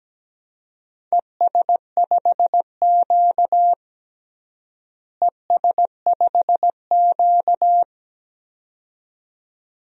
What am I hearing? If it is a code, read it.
Morse code, heard as a keyed tone, about 17 wpm, 700 Hz. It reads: ES5Q ES5Q
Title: DXCC-100 Call Signs - 4 Characters - Repeated 2x - 17wpm